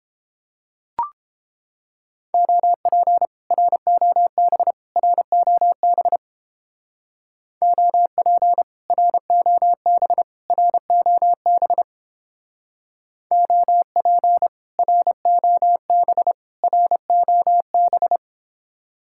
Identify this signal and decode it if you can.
Morse code, heard as a keyed tone, about 31 wpm, 700 Hz. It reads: OP ROB ROB OP ROB ROB OP ROB ROB